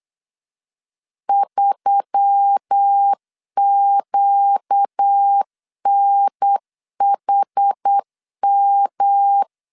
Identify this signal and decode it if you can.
Morse code, heard as a keyed tone, about 8 wpm, 800 Hz. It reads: 3QNHM
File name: LEZIONI / 6 PKX0 / 6 PKX 8wpm 3041